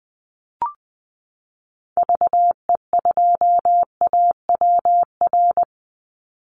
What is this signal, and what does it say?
Morse code, heard as a keyed tone, about 20 wpm, 700 Hz. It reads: VE2AWR
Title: Call Signs Repeated 3x - 20wpm